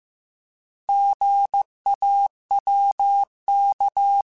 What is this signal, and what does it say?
Morse code, heard as a keyed tone, about 15 wpm, 800 Hz. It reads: GAWK
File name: Alphabetical / Alphabetical mp3s / Gs @13